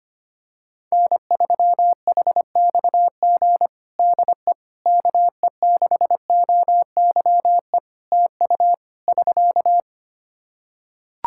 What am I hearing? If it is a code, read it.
Morse code, heard as a keyed tone, about 25 wpm, 700 Hz. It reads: N3HXG DE KE6OYE TU <SK>
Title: Practice QSO - Exchange 4 - Ending - 25wpm